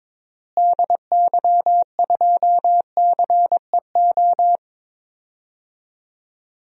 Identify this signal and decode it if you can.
Morse code, heard as a keyed tone, about 22 wpm, 700 Hz. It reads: DY2CEO
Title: DXCC-100 Call Signs - 6 Characters - Repeated 3x - 22wpm